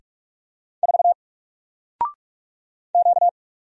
Morse code, heard as a keyed tone, about 44 wpm, 700 Hz. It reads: 4 Q
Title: Single Character (Letter:Number:Punctuation:Prosign) Alt 45wpm